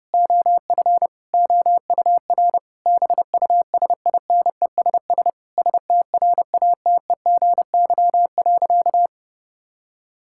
Morse code, heard as a keyed tone, about 30 wpm, 700 Hz. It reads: OF OUR BUSINESS STRATEGY.
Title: Sentences from Top 1000 Words - Review Part 6 - 30wpm